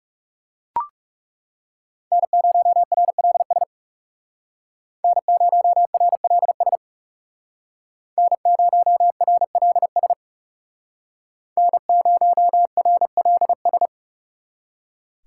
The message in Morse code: N0RLS N0RLS N0RLS N0RLS

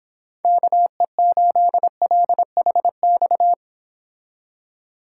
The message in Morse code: KE8LHX

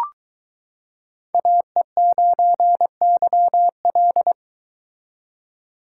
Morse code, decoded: AE9YL

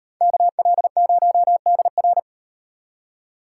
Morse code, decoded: KL0DR

